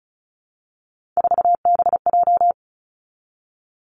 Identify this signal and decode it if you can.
Morse code, heard as a keyed tone, about 36 wpm, 700 Hz. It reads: 4BJ
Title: Three Challenging Letter-Number Combinations 35wpm